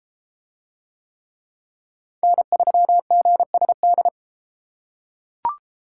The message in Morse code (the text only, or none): N3GSD